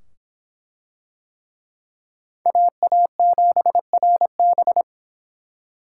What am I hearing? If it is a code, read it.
Morse code, heard as a keyed tone, about 24 wpm, 700 Hz. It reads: AA7RB